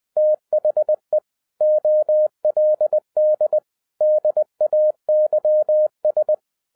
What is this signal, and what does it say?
Morse code, heard as a keyed tone, about 20 wpm, 610 Hz. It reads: THE OLD DAYS
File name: 2011-07-mp3 / n08 A LOT HAS CHANGED 20wpm